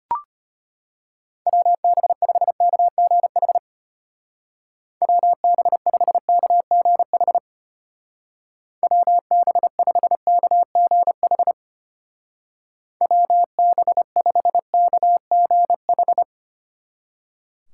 Morse code, heard as a keyed tone, about 31 wpm, 700 Hz. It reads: WB5KGH WB5KGH WB5KGH WB5KGH
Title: Speed Racing - USA 2x3 Call Signs 25wpm